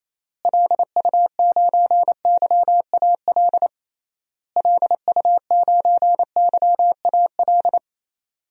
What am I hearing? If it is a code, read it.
Morse code, heard as a keyed tone, about 27 wpm, 700 Hz. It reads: LU9YAL LU9YAL
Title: DXCC-100 Call Signs - 6 Characters - Repeated 3x - 28wpm